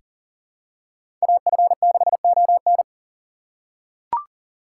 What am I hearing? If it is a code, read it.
Morse code, heard as a keyed tone, about 40 wpm, 700 Hz. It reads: AF6ON
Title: USA Call Signs - 6x Character Spacing - 40wpm